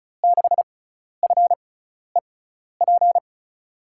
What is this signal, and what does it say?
Morse code, heard as a keyed tone, about 36 wpm, spaced as for 12 wpm, 700 Hz. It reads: 6FEP
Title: USA Call Signs - 6 Characters - 6x Character Spacing - 35wpm